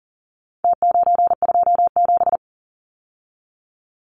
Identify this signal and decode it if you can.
Morse code, heard as a keyed tone, about 37 wpm, 700 Hz. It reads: T927
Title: Numbers - Contest Style - 4-Digit 40wpm